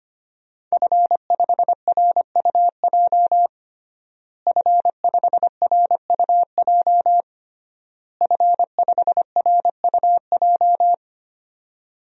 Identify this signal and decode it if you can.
Morse code, heard as a keyed tone, about 25 wpm, 700 Hz. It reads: F5RUJ F5RUJ F5RUJ
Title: DXCC-100 Call Signs - 5 Characters - Repeated 3x - 25wpm